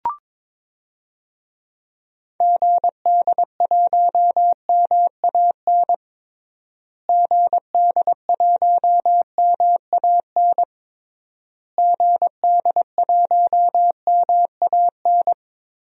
Morse code, heard as a keyed tone, about 22 wpm, 700 Hz. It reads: GD1MAN GD1MAN GD1MAN
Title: DXCC-100 Call Signs - 6 Characters - Repeated 3x - 22wpm